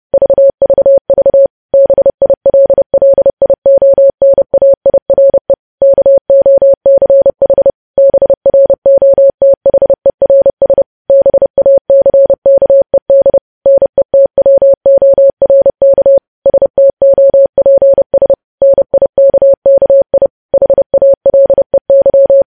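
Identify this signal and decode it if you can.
Morse code, heard as a keyed tone, about 30 wpm, 550 Hz. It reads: VVV BILLIONAIRE KOCH BROTHERS BACKED NETWORK STOPS NIKKI HALEY
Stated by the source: News headlines sent in Morse at 30 words per minute.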